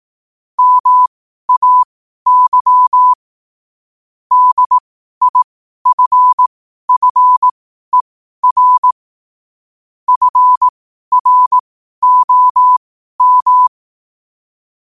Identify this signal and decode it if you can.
Morse code, heard as a keyed tone, about 18 wpm, spaced as for 12 wpm, 1000 Hz. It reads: MAY DIFFER FROM
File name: qotdfiles / 12wpm